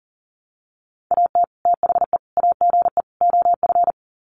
Morse code, heard as a keyed tone, about 40 wpm, 700 Hz. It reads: AT THE AGE OF
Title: Sets of 4 Words 40wpm